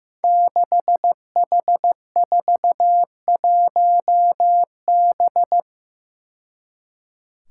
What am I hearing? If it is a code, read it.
Morse code, heard as a keyed tone, about 15 wpm, 700 Hz. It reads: 6H41B
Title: Five Challenging Letter-Number Combinations 15wpm